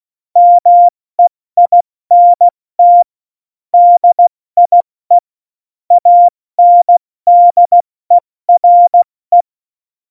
Morse code, heard as a keyed tone, about 16 wpm, 700 Hz. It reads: MEINT DIE ANDERE
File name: Morsekurs / Witze / Witz 12 Mai, 2020 70 F80